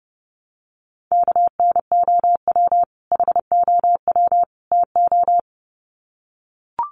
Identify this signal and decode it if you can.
Morse code, heard as a keyed tone, about 30 wpm, 700 Hz. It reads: KNOW HOW TO
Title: Sets of 3 Words 30wpm